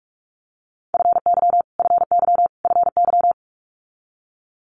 Morse code, heard as a keyed tone, about 46 wpm, 700 Hz. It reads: FY FY FY
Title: Multi-letter Phonemes Repeated 3x 45wpm